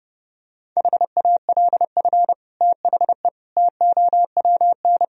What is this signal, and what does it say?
Morse code, heard as a keyed tone, about 30 wpm, 700 Hz. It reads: HALF THE TOWN
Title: Sentences from Top 1000 Words - Review Part 2 - 30wpm